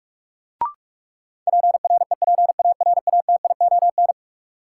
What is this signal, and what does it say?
Morse code, heard as a keyed tone, about 46 wpm, 700 Hz. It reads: PREPARATION
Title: Sentences from Top 1400 Words - New Word Drill - 45wpm